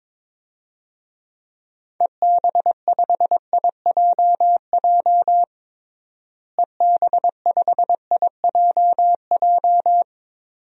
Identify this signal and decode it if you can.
Morse code, heard as a keyed tone, about 22 wpm, 700 Hz. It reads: EB5IJJ EB5IJJ
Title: DXCC-100 Call Signs - 6 Characters - Repeated 2x - 22wpm